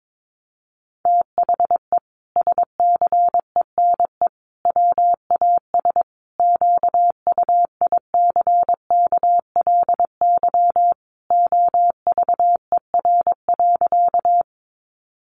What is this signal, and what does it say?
Morse code, heard as a keyed tone, about 22 wpm, 700 Hz. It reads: THE SCENE WAS QUICKLY OVER.
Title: Sentences from Top 1100 Words 22wpm